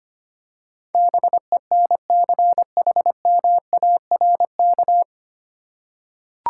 Morse code, heard as a keyed tone, about 25 wpm, 700 Hz. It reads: BENCHMARK
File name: Nine Letter Words - 25wpm